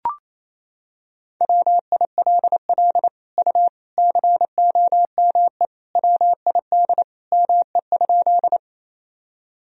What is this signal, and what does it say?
Morse code, heard as a keyed tone, about 27 wpm, 700 Hz. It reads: WILL U COME WID ME?
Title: Sentences from Top 200 Words and Abbreviations 28wpm